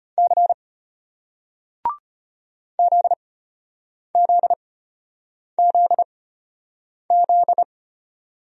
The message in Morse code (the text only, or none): C ZZZZ